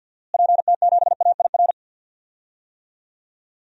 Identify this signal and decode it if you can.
Morse code, heard as a keyed tone, about 51 wpm, 700 Hz. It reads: PT7AIR
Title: DXCC-100 Call Signs - 6 Characters - Repeated 2x - 50wpm